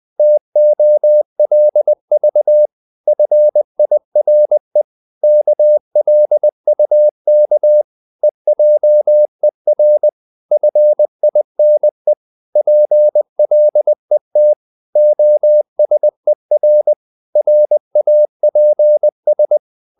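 Morse code, heard as a keyed tone, about 20 wpm, 600 Hz. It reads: TOLV FIRE KLUK EJER FINE PLET OSER RAPS
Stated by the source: Morse code practice of short Danish words at 20 wpm, from CW med Gnister.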